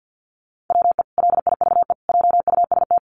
Morse code, heard as a keyed tone, about 50 wpm, 700 Hz. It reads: WE LIVE JUST
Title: Sentences from Top 1300 Words 50wpm